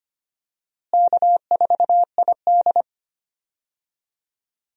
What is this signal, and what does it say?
Morse code, heard as a keyed tone, about 25 wpm, 700 Hz. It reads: K4ID
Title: USA Call Signs - 4 Characters - Repeated 2x - 25wpm